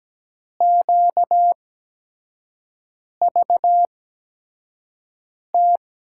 Morse code, heard as a keyed tone, about 17 wpm, spaced as for 5 wpm, 700 Hz. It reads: QVT
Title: USA Call Signs - 6 Characters - 8x Character Spacing - 17wpm